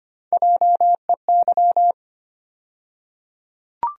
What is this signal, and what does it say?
Morse code, heard as a keyed tone, about 25 wpm, 700 Hz. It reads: JEY
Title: USA Call Sign Postfixes - 3 Letters 25wpm